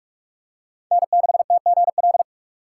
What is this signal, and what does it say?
Morse code, heard as a keyed tone, about 43 wpm, 700 Hz. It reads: N6TGL